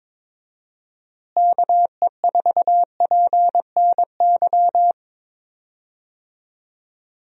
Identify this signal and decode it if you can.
Morse code, heard as a keyed tone, about 22 wpm, 700 Hz. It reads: KE4PNY
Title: USA Call Signs - 6 Characters - Repeated 3x - 22wpm